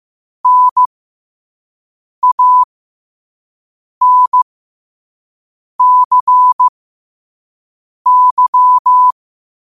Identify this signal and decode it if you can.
Morse code, heard as a keyed tone, about 15 wpm, spaced as for 5 wpm, 1000 Hz. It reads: NANCY
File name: qotdfiles / 5wpm